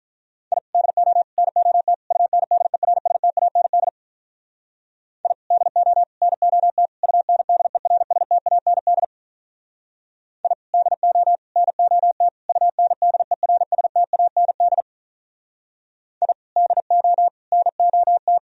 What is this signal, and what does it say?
Morse code, heard as a keyed tone, about 44 wpm, 700 Hz. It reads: I DO NOT UNDERSTAND I DO NOT UNDERSTAND I DO NOT UNDERSTAND I DO NOT
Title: Speed-Racing - Sets of 4 Words 35wpm